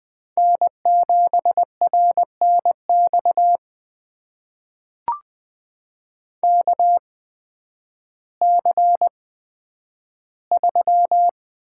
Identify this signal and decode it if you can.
Morse code, heard as a keyed tone, about 20 wpm, spaced as for 11 wpm, 700 Hz. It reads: N7RNX KC3
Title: USA Call Signs - 8x Character Spacing - 20wpm